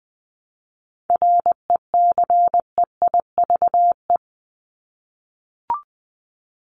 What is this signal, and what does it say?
Morse code, heard as a keyed tone, about 20 wpm, 700 Hz. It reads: RECEIVE
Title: Top 751-1000 Words 20wpm